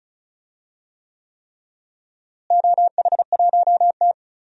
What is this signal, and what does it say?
Morse code, heard as a keyed tone, about 36 wpm, 700 Hz. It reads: OH1T
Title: DXCC-100 Call Signs - 4 Characters - Repeated 3x - 35wpm